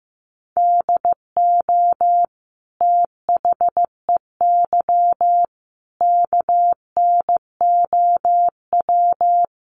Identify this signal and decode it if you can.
Morse code, heard as a keyed tone, about 15 wpm, 700 Hz. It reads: DO THEY KNOW
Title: Sentences from Top 100 Words 15wpm